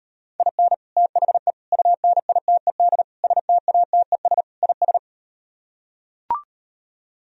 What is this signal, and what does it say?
Morse code, heard as a keyed tone, about 38 wpm, 700 Hz. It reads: IN THE UNITED STATES IS